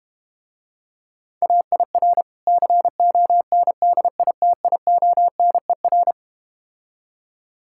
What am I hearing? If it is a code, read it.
Morse code, heard as a keyed tone, about 31 wpm, 700 Hz. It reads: AIR CONDITIONER
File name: practice / 25wpm / Difficult Words - Speed-Racing with Hints - 25wpm